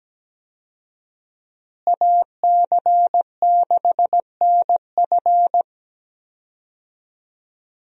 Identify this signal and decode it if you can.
Morse code, heard as a keyed tone, about 17 wpm, 700 Hz. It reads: AC6NF
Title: USA Call Signs - 5 Characters - Repeated 2x - 17wpm